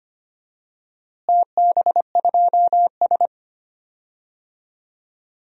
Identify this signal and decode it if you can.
Morse code, heard as a keyed tone, about 25 wpm, 700 Hz. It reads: TB2S
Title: DXCC-100 Call Signs - 4 Characters - Repeated 3x - 25wpm